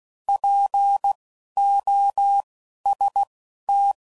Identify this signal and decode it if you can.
Morse code, heard as a keyed tone, about 16 wpm, spaced as for 11 wpm, 800 Hz. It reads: POST
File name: Prefixes / By First Letter / Ps